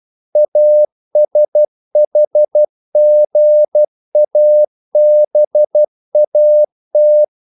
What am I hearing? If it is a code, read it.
Morse code, heard as a keyed tone, about 12 wpm, 600 Hz. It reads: ASHGABAT
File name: CW / Practice / NationalCapitals12